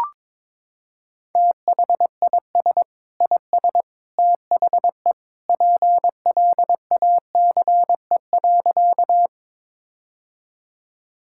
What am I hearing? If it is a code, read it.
Morse code, heard as a keyed tone, about 22 wpm, 700 Hz. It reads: THIS IS THE PLACE.
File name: practice / 22wpm / Sentences with 3-Word Phrases - 22wpm